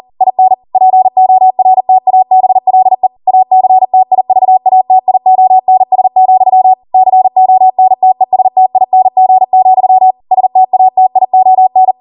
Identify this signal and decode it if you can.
Morse code, heard as a keyed tone, about 40 wpm, 750 Hz. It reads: IN PORTABLE ACTIVATIONS, CONTESTING, STATION